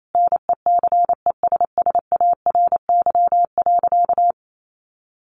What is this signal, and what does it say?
Morse code, heard as a keyed tone, about 28 wpm, 700 Hz. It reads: NECESSARY.
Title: Sentences from Top 1400 Words 28wpm